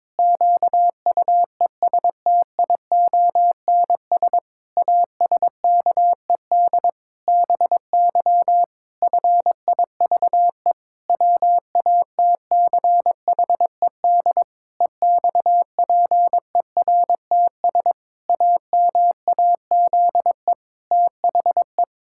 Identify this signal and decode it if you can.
Morse code, heard as a keyed tone, about 22 wpm, 700 Hz. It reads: QUESTIONS ASKED BY FIVE WATCHED EXPERTS AMAZE THE
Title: Pangram Challenge v2 - 22wpm